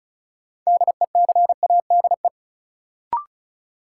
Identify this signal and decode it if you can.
Morse code, heard as a keyed tone, about 36 wpm, 700 Hz. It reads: DECADE
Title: Sets of 5 Words - New Word Drill - 35wpm